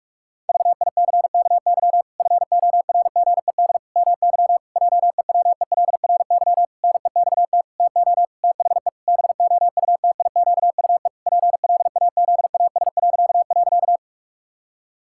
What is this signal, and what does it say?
Morse code, heard as a keyed tone, about 48 wpm, 700 Hz. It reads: VICKY FORGED MY JEWELRY NEXT TO THE BOUTIQUE PLAZAS..
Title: Pangram Challenge v2 - 45wpm